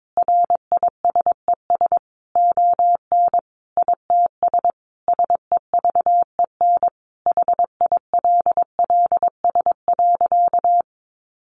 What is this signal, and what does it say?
Morse code, heard as a keyed tone, about 22 wpm, 700 Hz. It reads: RISES ON ITS SEVEN HILLS.
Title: A Sound Of Thunder 22wpm